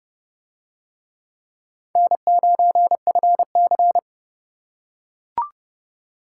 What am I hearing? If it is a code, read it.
Morse code, heard as a keyed tone, about 28 wpm, 700 Hz. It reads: N9FC